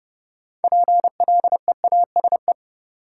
Morse code, heard as a keyed tone, about 32 wpm, 700 Hz. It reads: PLEASE
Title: Sentences from Top 1500 Words - New Word Drill - 30wpm